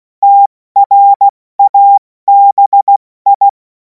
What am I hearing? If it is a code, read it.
Morse code, heard as a keyed tone, about 16 wpm, 800 Hz. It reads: TRABI
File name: Morsekurs / Witze / Witz 21. April 2026 70 F80